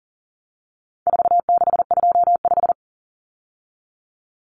Morse code, heard as a keyed tone, about 40 wpm, 700 Hz. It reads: 4625
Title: Common Cut Numbers - 4-Digit 40wpm